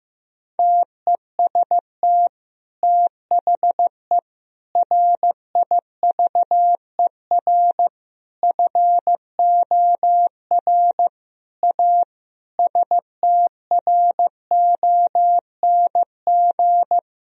Morse code, heard as a keyed tone, about 16 wpm, 700 Hz. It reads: TEST THE RIVER FOR A STRONG
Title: Sentences from Top 1000 Words - Review Part 4 - 15wpm